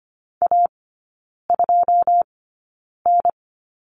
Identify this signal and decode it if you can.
Morse code, heard as a keyed tone, about 25 wpm, spaced as for 9 wpm, 690 Hz. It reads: A2N
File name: Lesson 14e - Number 2 - Three Characters 25wpm